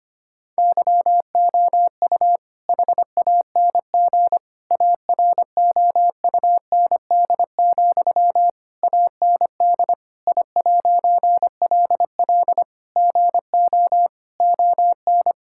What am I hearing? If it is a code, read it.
Morse code, heard as a keyed tone, about 25 wpm, 700 Hz. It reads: YOU HANG AROUND, AND I'LL GO ON
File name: practice / 25wpm / All-Age Appropriate Jokes - 25wpm